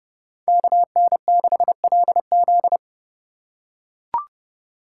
Morse code, heard as a keyed tone, about 30 wpm, 700 Hz. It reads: KN6LZ
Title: USA Call Signs - 5 Characters - 2x Character Spacing - 30wpm